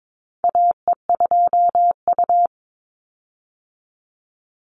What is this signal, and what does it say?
Morse code, heard as a keyed tone, about 22 wpm, 700 Hz. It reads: AE2U
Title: Call Signs - 22wpm